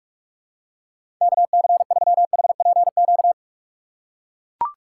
Morse code, heard as a keyed tone, about 43 wpm, 700 Hz. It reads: KC3HPQ
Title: USA Call Signs - 6 Characters - 2x Character Spacing - 45wpm